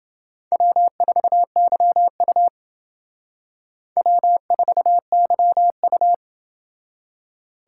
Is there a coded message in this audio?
Yes: Morse code from 0.5 to 6.1 s, heard as a keyed tone, about 29 wpm, 700 Hz. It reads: W4YU W4YU